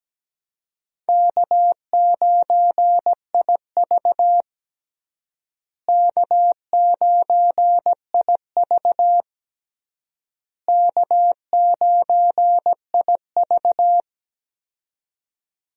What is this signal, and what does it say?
Morse code, heard as a keyed tone, about 17 wpm, 700 Hz. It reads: K9IV K9IV K9IV
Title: USA Call Signs - 4 Characters - Repeated 3x - 17wpm